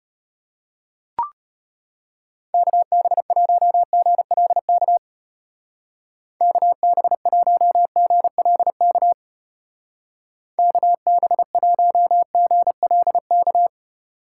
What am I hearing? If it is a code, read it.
Morse code, heard as a keyed tone, about 32 wpm, 700 Hz. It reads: KB1GLK KB1GLK KB1GLK